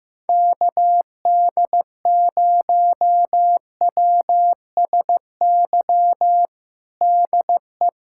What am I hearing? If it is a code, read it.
Morse code, heard as a keyed tone, about 16 wpm, 700 Hz. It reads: KD0WSY DE
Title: USA Callsign DE Callsign 15wpm